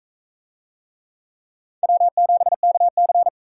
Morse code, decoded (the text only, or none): W7KC